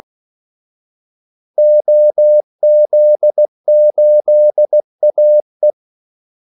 Morse code, heard as a keyed tone, about 16 wpm, 600 Hz. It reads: OZ8AE
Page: Kaldesignaler 16 wpm